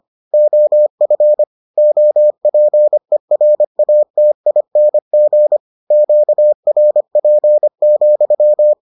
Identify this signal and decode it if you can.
Morse code, heard as a keyed tone, about 25 wpm, 600 Hz. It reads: OF OPERATING QRP,